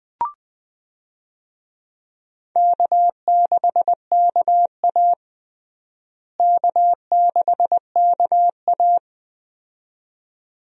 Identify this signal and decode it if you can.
Morse code, heard as a keyed tone, about 20 wpm, 700 Hz. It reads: K6KA K6KA
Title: USA Call Signs - 4 Characters - Repeated 2x - 20wpm